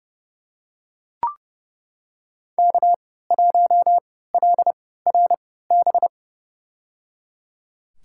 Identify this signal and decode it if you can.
Morse code, heard as a keyed tone, about 30 wpm, spaced as for 17 wpm, 700 Hz. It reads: K1LRB
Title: USA Call Signs - 5 Characters - 3x Character Spacing - 30wpm